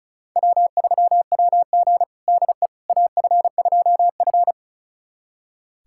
Morse code, heard as a keyed tone, about 36 wpm, 700 Hz. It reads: W3WG DE AF2F
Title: USA Callsign DE Callsign - 4 Characters - 35wpm